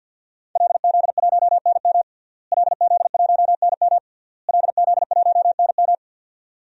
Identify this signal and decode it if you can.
Morse code, heard as a keyed tone, about 52 wpm, 700 Hz. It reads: LZ1NM LZ1NM LZ1NM
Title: DXCC-100 Call Signs - 5 Characters - Repeated 3x - 50wpm